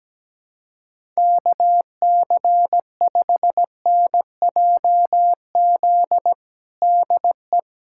Morse code, heard as a keyed tone, about 17 wpm, 700 Hz. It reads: KC5NJZ DE
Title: USA Callsign DE Callsign 17wpm